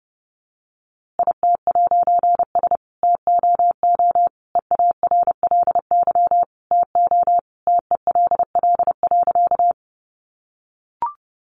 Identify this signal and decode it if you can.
Morse code, heard as a keyed tone, about 30 wpm, 700 Hz. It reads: IT'S TOO EARLY TO TELL.
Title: A Sound Of Thunder 30wpm